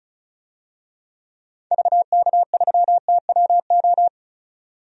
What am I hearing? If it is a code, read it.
Morse code, heard as a keyed tone, about 34 wpm, 700 Hz. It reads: VK3TWO
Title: DXCC-100 Call Signs - 6 Characters - 35wpm